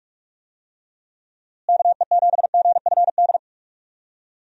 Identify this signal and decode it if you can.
Morse code, heard as a keyed tone, about 45 wpm, 700 Hz. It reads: KE7GFD
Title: USA Call Signs - 6 Characters - Repeated 3x - 45wpm